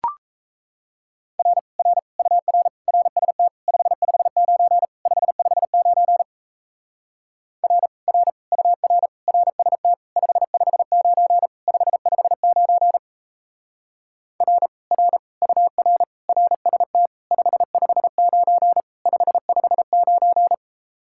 Morse code, heard as a keyed tone, about 37 wpm, 700 Hz. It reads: R R UR RST 559 559 R R UR RST 559 559 R R UR RST 559 559